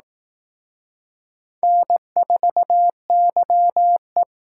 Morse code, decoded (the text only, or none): N4YE